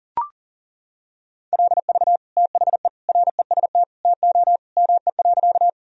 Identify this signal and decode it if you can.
Morse code, heard as a keyed tone, about 40 wpm, 700 Hz. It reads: LV THE REST TO ME.
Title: Sentences from Top 500 Words and Abbreviations 40wpm